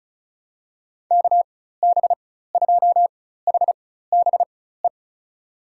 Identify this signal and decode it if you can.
Morse code, heard as a keyed tone, about 36 wpm, spaced as for 16 wpm, 700 Hz. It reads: KB2HBE
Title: USA Call Signs - 6 Characters - 4x Character Spacing - 35wpm